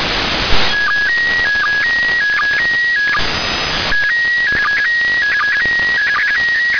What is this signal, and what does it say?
SSTV (slow-scan television), PD 180.
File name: iss 14h24 utc 145mhz800 sstv